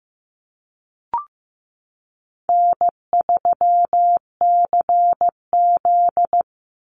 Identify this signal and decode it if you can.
Morse code, heard as a keyed tone, about 15 wpm, 700 Hz. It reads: N3CZ
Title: Easy Call Signs Repeated 3x - 15wpm